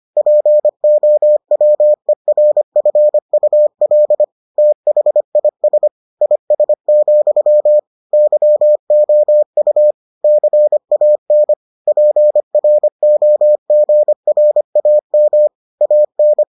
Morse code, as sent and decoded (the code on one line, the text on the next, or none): .--. --- .-- . .-. ..-. ..- .-.. / - .... .. ... / .. ... --..-- / -.-- --- ..- / -.-. .- -. / .--. .-. --- --. .-. .- -- / .- -.
POWERFUL THIS IS, YOU CAN PROGRAM AN